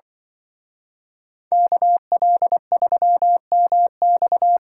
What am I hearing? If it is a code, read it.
Morse code, heard as a keyed tone, about 24 wpm, 700 Hz. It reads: KL3MX
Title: Speed Racing - USA 2x2 Call Signs 20wpm